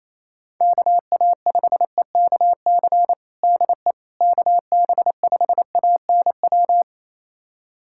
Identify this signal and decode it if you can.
Morse code, heard as a keyed tone, about 28 wpm, 700 Hz. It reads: KA5EKC DE KB5ANW
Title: USA Callsign DE Callsign - 6 Characters - 28wpm